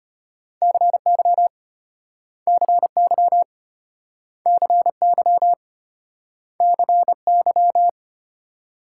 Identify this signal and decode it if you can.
Morse code, heard as a keyed tone, about 32 wpm, 700 Hz. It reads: CY CY CY CY